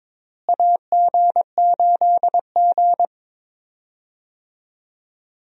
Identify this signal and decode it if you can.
Morse code, heard as a keyed tone, about 21 wpm, 700 Hz. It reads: AG8G
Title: USA Call Signs - 4 Characters - Repeated 3x - 22wpm